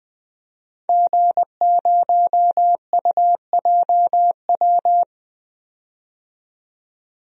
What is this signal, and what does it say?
Morse code, heard as a keyed tone, about 20 wpm, 700 Hz. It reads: G0UJW
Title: DXCC-100 Call Signs - 5 Characters - 20wpm